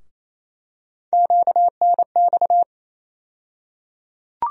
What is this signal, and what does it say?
Morse code, heard as a keyed tone, about 28 wpm, 700 Hz. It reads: QNX
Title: USA Call Sign Postfixes - 3 Letters 28wpm